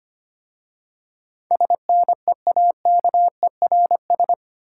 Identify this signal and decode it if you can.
Morse code, heard as a keyed tone, about 25 wpm, 700 Hz. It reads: SNEAKERS
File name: practice / 25wpm / Easy Words with Hints - 25wpm